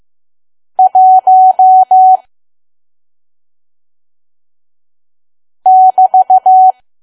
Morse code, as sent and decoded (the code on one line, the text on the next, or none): .---- / -...-
1 <BT>